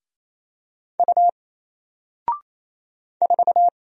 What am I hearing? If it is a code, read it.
Morse code, heard as a keyed tone, about 27 wpm, 700 Hz. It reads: U 4
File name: Single-Letter-Number-28wpm